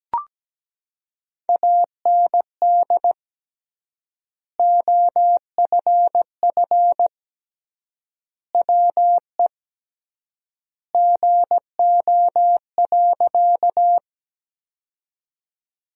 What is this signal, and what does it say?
Morse code, heard as a keyed tone, about 17 wpm, 700 Hz. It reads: AND OFF WE GO.
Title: 3x Word-Spacing - Sentences from Top 200 Words - 17wpm